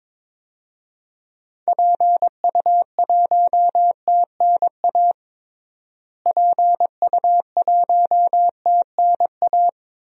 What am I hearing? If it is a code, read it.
Morse code, heard as a keyed tone, about 22 wpm, 700 Hz. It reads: PU1TNA PU1TNA
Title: DXCC-100 Call Signs - 6 Characters - Repeated 2x - 22wpm